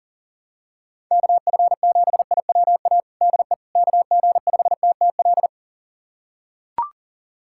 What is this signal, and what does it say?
Morse code, heard as a keyed tone, about 40 wpm, 700 Hz. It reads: KF7IWA DE KG5TTL
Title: USA Callsign DE Callsign 40wpm